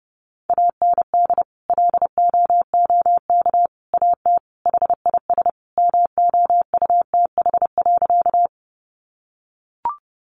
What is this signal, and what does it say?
Morse code, heard as a keyed tone, about 28 wpm, 700 Hz. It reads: AND LOOK AT HIS MOUTH.
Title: Sentences from Top 1500 Words 30wpm